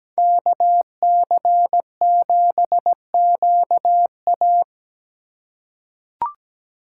Morse code, heard as a keyed tone, about 17 wpm, 700 Hz. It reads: KC7QA
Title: USA Call Signs - 5 Characters - 3x Character Spacing - 17wpm